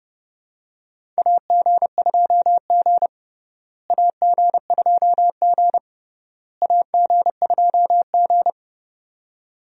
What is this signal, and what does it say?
Morse code, heard as a keyed tone, about 32 wpm, 700 Hz. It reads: AG2G AG2G AG2G
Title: USA Call Signs - 4 Characters - Repeated 3x - 30wpm